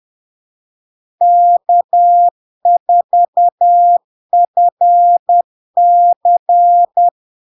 Morse code, heard as a keyed tone, about 10 wpm, 700 Hz. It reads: K4FC
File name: LARK-FD-2022-Log-in-CW-10wpm-700-Hz